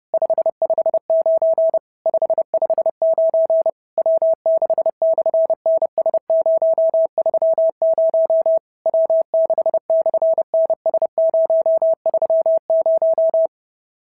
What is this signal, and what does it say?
Morse code, heard as a keyed tone, about 30 wpm, 650 Hz. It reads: 559 559 W6/NS030 W6/NS030
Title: SOTA QSO Practice 30wpm